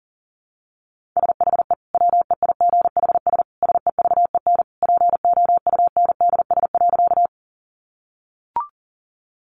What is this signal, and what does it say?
Morse code, heard as a keyed tone, about 40 wpm, 700 Hz. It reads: SHE WEIGHS SEVEN POUNDS.
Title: Easy Phrases 40wpm